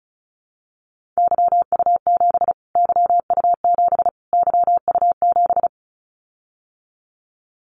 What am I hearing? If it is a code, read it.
Morse code, heard as a keyed tone, about 34 wpm, 700 Hz. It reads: YU7 YU7 YU7
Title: Callsign Prefixes for Top 25 DXCC - Repeated 3x - 35wpm